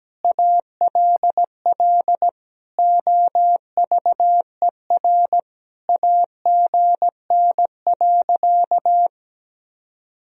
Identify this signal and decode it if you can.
Morse code, heard as a keyed tone, about 17 wpm, 700 Hz. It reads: ALL OVER AGN.
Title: Sentences from Top 500 Words and Abbreviations 17wpm